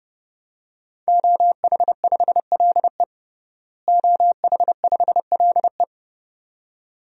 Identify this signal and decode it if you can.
Morse code, heard as a keyed tone, about 32 wpm, 700 Hz. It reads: OH5LE OH5LE
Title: DXCC-100 Call Signs - 5 Characters - Repeated 2x - 30wpm